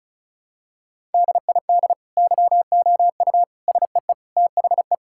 Morse code, heard as a keyed tone, about 34 wpm, 700 Hz. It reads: DID YOU SEE THE
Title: Sentences from Top 1000 Words - Review Part 5 - 35wpm